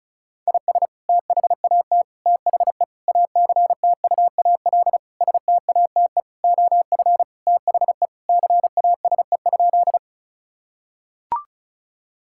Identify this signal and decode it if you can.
Morse code, heard as a keyed tone, about 36 wpm, 700 Hz. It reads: IS THAT THE ACTUAL STATE OF THE CASE?
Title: Sentences from Top 1000 Words - Review Part 5 - 35wpm